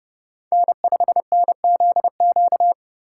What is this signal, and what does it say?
Morse code, heard as a keyed tone, about 32 wpm, 700 Hz. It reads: N5NZQ
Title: USA Call Signs - 3x Character Spacing - 30wpm